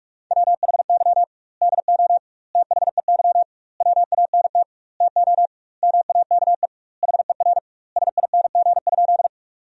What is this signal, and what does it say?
Morse code, heard as a keyed tone, about 41 wpm, 700 Hz. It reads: WHY DO THEY WANT TO MAKE HER SING?
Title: 2x Word-Spacing - Sentences from Top 400 Words - 45wpm